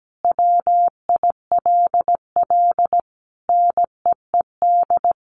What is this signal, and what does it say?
Morse code, heard as a keyed tone, about 17 wpm, 700 Hz. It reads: WILL NEED
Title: Sentences from Top 1500 Words 17wpm